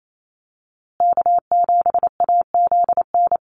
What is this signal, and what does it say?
Morse code, heard as a keyed tone, about 28 wpm, 700 Hz. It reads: K7AZN
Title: USA Call Signs Repeated 2x 28wpm